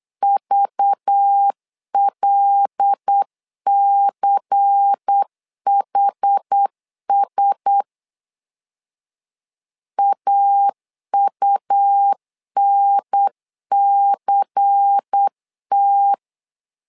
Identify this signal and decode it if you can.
Morse code, heard as a keyed tone, about 8 wpm, 800 Hz. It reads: VLCHS AUNCT
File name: LEZIONI / 5 LBVC / 5 LBVC 8wpm 2935